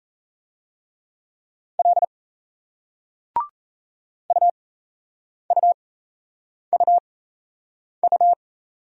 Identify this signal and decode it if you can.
Morse code, heard as a keyed tone, about 37 wpm, spaced as for 8 wpm, 700 Hz. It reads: L UUUU